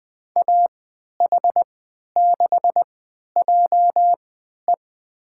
Morse code, heard as a keyed tone, about 20 wpm, spaced as for 11 wpm, 700 Hz. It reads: AH6JE
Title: USA Call Signs - 5 Characters - 3x Character Spacing - 20wpm